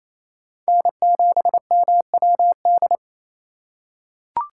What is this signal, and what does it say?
Morse code, heard as a keyed tone, about 28 wpm, 700 Hz. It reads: N7MWD
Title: USA Call Signs - 5 Characters - 4x Character Spacing - 28wpm